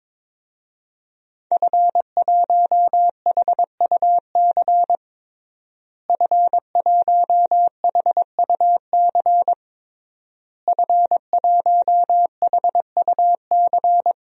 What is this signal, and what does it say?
Morse code, heard as a keyed tone, about 22 wpm, 700 Hz. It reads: F1HUC F1HUC F1HUC
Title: DXCC-100 Call Signs - 5 Characters - Repeated 3x - 22wpm